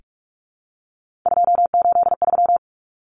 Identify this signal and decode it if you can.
Morse code, heard as a keyed tone, about 44 wpm, 700 Hz. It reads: 283
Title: Common Cut Numbers - 3-Digit 45wpm